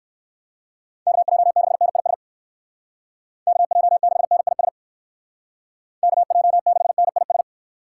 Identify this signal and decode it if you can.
Morse code, heard as a keyed tone, about 54 wpm, 700 Hz. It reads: KJ6NIS KJ6NIS KJ6NIS
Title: Speed Racing - USA 2x3 Call Signs 45wpm